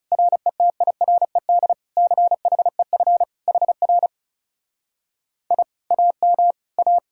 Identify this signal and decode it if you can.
Morse code, heard as a keyed tone, about 37 wpm, 700 Hz. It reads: RETIRED CHEF HR I AM A
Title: Speed-Racing Common QSO Phrases Encore - 30wpm